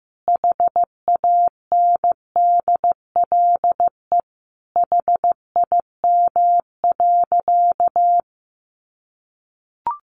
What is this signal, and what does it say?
Morse code, heard as a keyed tone, about 15 wpm, 700 Hz. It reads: HANDLE HIM.
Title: Sentences from Top 1100 Words 15wpm